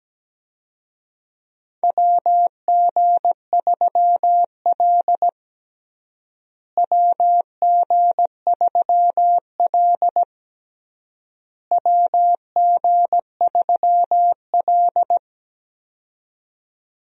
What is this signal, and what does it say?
Morse code, heard as a keyed tone, about 17 wpm, 700 Hz. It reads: WG3L WG3L WG3L
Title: USA Call Signs - 4 Characters - Repeated 3x - 17wpm